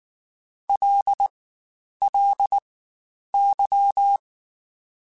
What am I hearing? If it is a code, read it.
Morse code, heard as a keyed tone, about 19 wpm, spaced as for 9 wpm, 800 Hz. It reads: LLY
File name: Spaced Characters / Spaced Characters-mp3s / Weather 4 10 sp @18